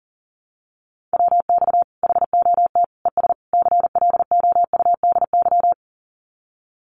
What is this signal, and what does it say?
Morse code, heard as a keyed tone, about 40 wpm, 700 Hz. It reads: WX HOT ES CLOUDY
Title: Common QSO Phrases 40wpm